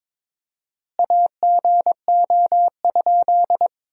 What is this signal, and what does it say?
Morse code, heard as a keyed tone, about 21 wpm, 700 Hz. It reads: AGO?
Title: 4x Word-Spacing - Sentences from Top 500 Words - 22wpm